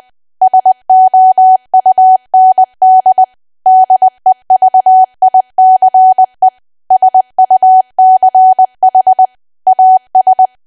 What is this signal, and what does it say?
Morse code, heard as a keyed tone, about 20 wpm, 750 Hz. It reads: SOUND DEVICE SUCH AS
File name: cwa / W1AW-Level3 / 20WPM